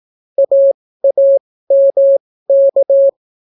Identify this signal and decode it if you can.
Morse code, heard as a keyed tone, about 17 wpm, spaced as for 14 wpm, 550 Hz. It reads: AAMK